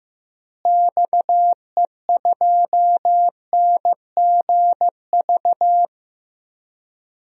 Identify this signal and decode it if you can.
Morse code, heard as a keyed tone, about 15 wpm, 700 Hz. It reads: XE2NGV
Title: DXCC-100 Call Signs - 6 Characters - Repeated 3x - 15wpm